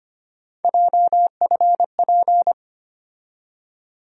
Morse code, heard as a keyed tone, about 25 wpm, 700 Hz. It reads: JFP
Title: USA Call Sign Postfixes - 3 Letters 25wpm